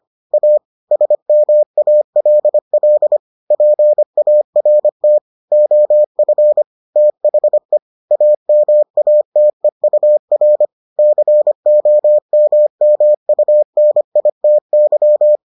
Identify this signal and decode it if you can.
Morse code, heard as a keyed tone, about 25 wpm, 600 Hz. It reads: A SMALL PART OF THE AMATEUR COMMUNITY